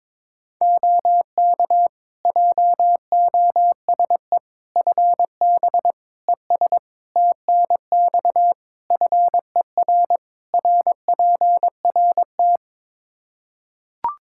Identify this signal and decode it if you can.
Morse code, heard as a keyed tone, about 22 wpm, 700 Hz. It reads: OK JOSE FB ES TNX FER RPRT
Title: K1USN SST QSO Practice 22wpm